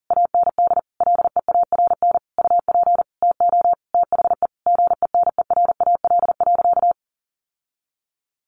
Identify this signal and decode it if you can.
Morse code, heard as a keyed tone, about 40 wpm, 700 Hz. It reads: AND LEARN UP TO THE GENERAL.
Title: Sentences from Top 1500 Words 40wpm